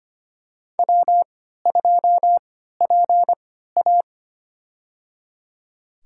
Morse code, heard as a keyed tone, about 25 wpm, spaced as for 14 wpm, 700 Hz. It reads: W2PA
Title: USA Call Signs - 4 Characters - 3x Character Spacing - 25wpm